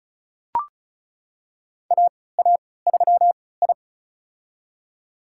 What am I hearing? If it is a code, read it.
Morse code, heard as a keyed tone, about 36 wpm, spaced as for 20 wpm, 690 Hz. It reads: AA3I
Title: USA Call Signs - 4 Characters - 3x Character Spacing - 35wpm